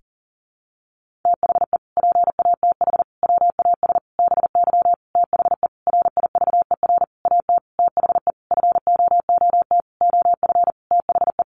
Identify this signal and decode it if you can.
Morse code, heard as a keyed tone, about 40 wpm, 700 Hz. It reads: THE PATH WAS BY THE RIVER AT THE FOOT OF THE
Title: Sentences from Top 1400 Words 40wpm